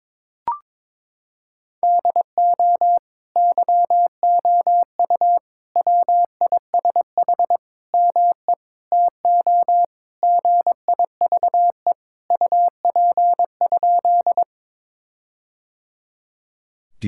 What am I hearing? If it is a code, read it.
Morse code, heard as a keyed tone, about 22 wpm, 700 Hz. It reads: DO YOU WISH ME TO GIVE UP?
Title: Sentences from Top 1600 Words 22wpm